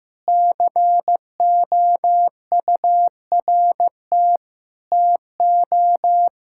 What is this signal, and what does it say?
Morse code, heard as a keyed tone, about 16 wpm, 700 Hz. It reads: COURT TO
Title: Sentences from Top 1600 Words 15wpm